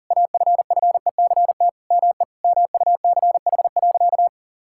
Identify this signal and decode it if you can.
Morse code, heard as a keyed tone, about 40 wpm, 700 Hz. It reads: AFFECT ME MUCH.
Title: Sentences from Top 1000 Words - Review Part 6 - 40wpm